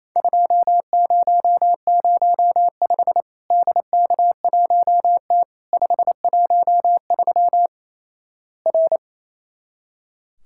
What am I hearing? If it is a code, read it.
Morse code, heard as a keyed tone, about 28 wpm, 700 Hz. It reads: 2005 DK1T 513 R
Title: Worked All Europe DX Contest - QTC Exchange - 28wpm